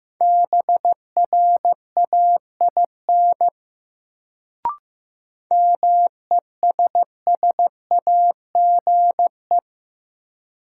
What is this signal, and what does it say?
Morse code, heard as a keyed tone, about 15 wpm, 700 Hz. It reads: BRAIN MESSAGE
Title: Sentences from Top 1100 Words - New Word Drill Alt - 15wpm